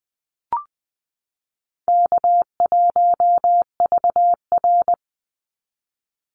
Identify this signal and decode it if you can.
Morse code, heard as a keyed tone, about 20 wpm, 700 Hz. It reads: K1VR
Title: Easy Call Signs Repeated 3x - 20wpm